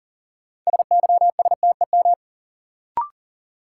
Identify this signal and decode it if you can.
Morse code, heard as a keyed tone, about 40 wpm, 700 Hz. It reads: SYSTEM
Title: Sentences from Top 1000 Words - Review Part 3 - New Word Drill - 40wpm